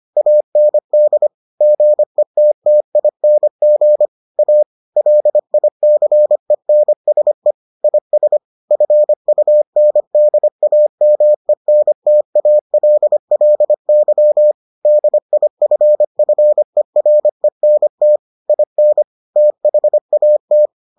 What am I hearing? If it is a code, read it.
Morse code, heard as a keyed tone, about 25 wpm, 600 Hz. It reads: AND GETTING A LICENSE IS FUNDAMENTALLY DIFFERENT IN THAT